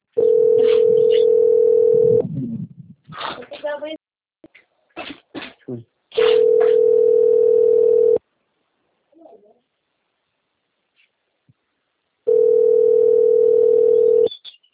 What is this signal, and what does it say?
Ringback tone (North America), 440 + 480 Hz, with speech.
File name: out- - - - - .25307